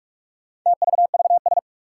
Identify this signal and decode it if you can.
Morse code, heard as a keyed tone, about 46 wpm, 700 Hz. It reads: TVVS
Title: Four Challenging Letter-Number Combinations 45wpm